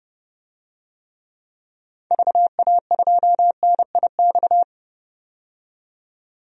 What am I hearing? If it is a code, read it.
Morse code, heard as a keyed tone, about 31 wpm, 700 Hz. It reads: VA2NIX
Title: DXCC-100 Call Signs - 6 Characters - 30wpm